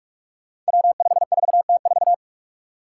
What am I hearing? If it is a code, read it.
Morse code, heard as a keyed tone, about 44 wpm, 700 Hz. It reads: W54T4
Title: Five Challenging Letter-Number Combinations 45wpm